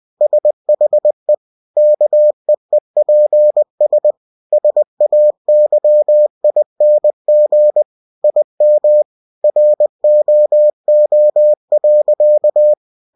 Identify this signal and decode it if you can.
Morse code, heard as a keyed tone, about 20 wpm, 600 Hz. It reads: SHE KEEPS SAYING IM ROO.
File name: cw-books / winnie-the-pooh / winnie-20wpm-page-0073